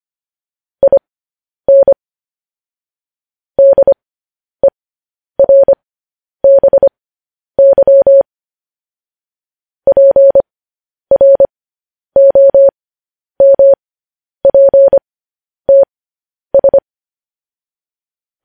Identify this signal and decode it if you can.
Morse code, heard as a keyed tone, about 25 wpm, spaced as for 10 wpm, 550 Hz. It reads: IN DERBY PROMPTS
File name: podcast / News headlines-1x-25x10